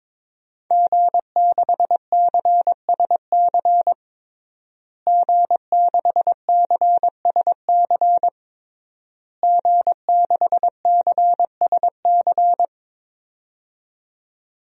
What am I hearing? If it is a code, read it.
Morse code, heard as a keyed tone, about 22 wpm, 700 Hz. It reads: G6CSC G6CSC G6CSC
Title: DXCC-100 Call Signs - 5 Characters - Repeated 3x - 22wpm